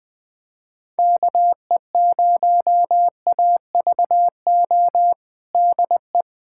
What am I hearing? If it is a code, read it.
Morse code, heard as a keyed tone, about 20 wpm, 700 Hz. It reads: KE0AVO DE
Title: USA Callsign DE Callsign 20wpm